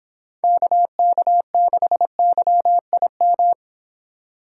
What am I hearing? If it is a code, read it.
Morse code, heard as a keyed tone, about 31 wpm, 700 Hz. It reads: KK6YIM